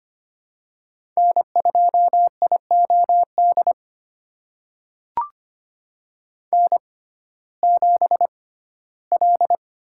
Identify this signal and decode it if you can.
Morse code, heard as a keyed tone, about 25 wpm, spaced as for 15 wpm, 700 Hz. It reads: N2IOD N7L
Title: USA Call Signs - 5 Characters - 6x Character Spacing - 25wpm